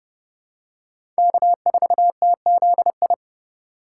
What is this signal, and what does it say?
Morse code, heard as a keyed tone, about 28 wpm, 700 Hz. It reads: K4TZI